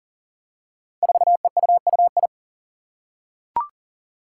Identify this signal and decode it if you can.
Morse code, heard as a keyed tone, about 40 wpm, 700 Hz. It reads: 4EUUI
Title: Five Challenging Letter-Number Combinations 40wpm